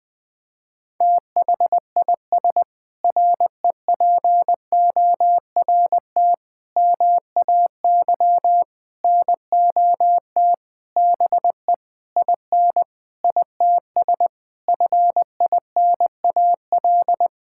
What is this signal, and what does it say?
Morse code, heard as a keyed tone, about 20 wpm, 700 Hz. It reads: THIS REPORT MAY NOT BE IN ITS FINAL
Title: Sentences with 7-Word Phrases - 20wpm